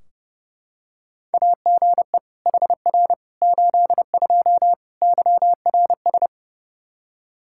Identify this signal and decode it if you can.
Morse code, heard as a keyed tone, about 28 wpm, 700 Hz. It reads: AGE HR 82 YRS
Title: K1USN SST QSO Practice 30wpm